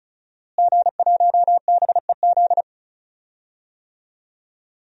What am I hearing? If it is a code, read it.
Morse code, heard as a keyed tone, about 36 wpm, 700 Hz. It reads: G1BEZ
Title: DXCC-100 Call Signs - 5 Characters - Repeated 2x - 35wpm